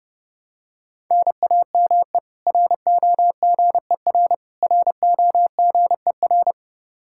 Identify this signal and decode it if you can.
Morse code, heard as a keyed tone, about 30 wpm, 700 Hz. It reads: NAME ROGER ROGER
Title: Practice QSO - Exchange 1 - Beginning - 30wpm